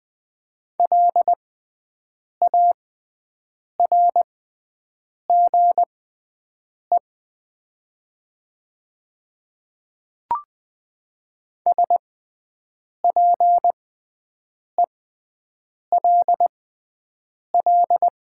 Morse code, heard as a keyed tone, about 20 wpm, spaced as for 7 wpm, 700 Hz. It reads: LARGE SPELL
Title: 6x Character Spacing - Top 200 Words - 20wpm